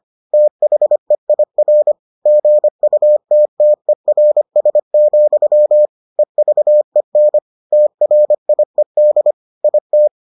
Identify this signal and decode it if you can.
Morse code, heard as a keyed tone, about 25 wpm, 600 Hz. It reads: THEIR GUTTERS, EVEN TRIED IT